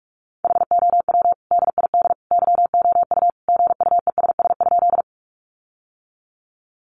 Morse code, heard as a keyed tone, about 44 wpm, 700 Hz. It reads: HOW DID YOU GUESS?
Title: Sentences from Top 1500 Words 45wpm